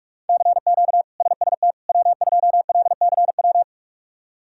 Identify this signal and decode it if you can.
Morse code, heard as a keyed tone, about 46 wpm, 700 Hz. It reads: CQ SST W2LCW
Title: K1USN SST QSO Practice - 45wpm